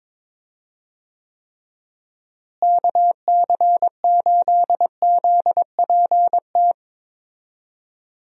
Morse code, heard as a keyed tone, about 22 wpm, 700 Hz. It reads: KC8ZPT
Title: USA Call Signs - 6 Characters - Repeated 3x - 22wpm